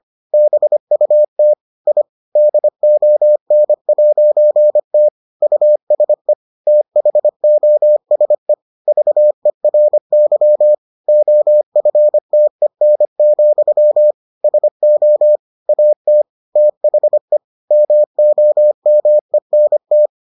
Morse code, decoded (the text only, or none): BUT I DON'T USE THOSE VERY OFTEN, SO AT THE MOMENT